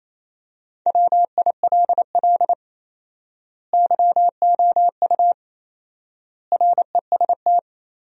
Morse code, heard as a keyed tone, about 28 wpm, 700 Hz. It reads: WILL YOU REST
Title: 4x Word-Spacing - Sentences from Top 500 Words - 28wpm